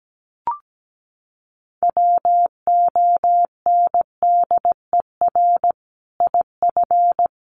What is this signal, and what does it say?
Morse code, heard as a keyed tone, about 17 wpm, 700 Hz. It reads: WONDER IF
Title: Sentences from Top 500 Words 17wpm